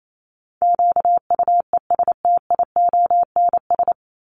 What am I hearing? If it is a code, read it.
Morse code, heard as a keyed tone, about 28 wpm, 700 Hz. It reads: QUESTIONS
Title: ES:S Suffix - Top 100 Words with Base Words 28wpm